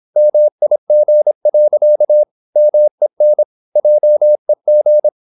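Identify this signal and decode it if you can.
Morse code, heard as a keyed tone, about 26 wpm, 600 Hz. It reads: MIG. MEN JEG
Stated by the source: High-speed (QRQ) Morse at 26 wpm.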